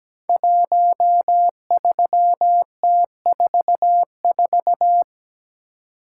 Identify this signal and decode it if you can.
Morse code, heard as a keyed tone, about 17 wpm, 700 Hz. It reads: 13T44
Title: Code Groups - 5 Letter:Numbers - 17wpm